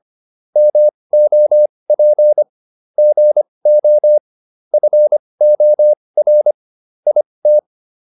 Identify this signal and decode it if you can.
Morse code, heard as a keyed tone, about 25 wpm, spaced as for 20 wpm, 600 Hz. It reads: MOP GO FOR IT